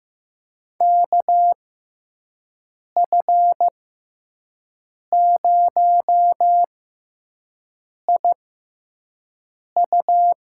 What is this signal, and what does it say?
Morse code, heard as a keyed tone, about 15 wpm, spaced as for 5 wpm, 700 Hz. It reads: KF0IU
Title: USA Call Signs - 5 Characters - 6x Character Spacing - 15wpm